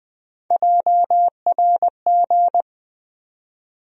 Morse code, heard as a keyed tone, about 20 wpm, 700 Hz. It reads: JRG
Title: USA Call Sign Postfixes - 3 Letters 20wpm